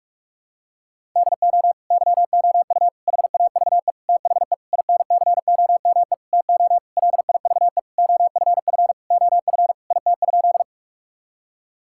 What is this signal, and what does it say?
Morse code, heard as a keyed tone, about 42 wpm, 700 Hz. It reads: DO YOU HAVE THE INCOME TO LIVE OFF OF IT?
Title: Sentences from Top 1000 Words - Review Part 5 - 45wpm